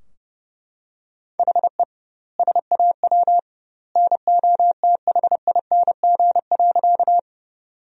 Morse code, heard as a keyed tone, about 32 wpm, 700 Hz. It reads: HE SAW NOTHING.
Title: 2x Word-Spacing - Sentences from Top 500 Words - 30wpm